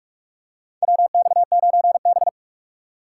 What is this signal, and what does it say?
Morse code, heard as a keyed tone, about 46 wpm, 700 Hz. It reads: WX9B
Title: USA Call Signs - 2x Character Spacing - 45wpm